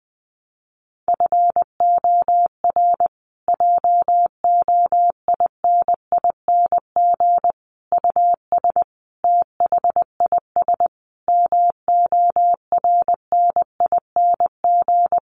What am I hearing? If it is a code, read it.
Morse code, heard as a keyed tone, about 19 wpm, 700 Hz. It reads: FOR JOINING US THIS MORNING
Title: Sets of 5 Words Encore 20wpm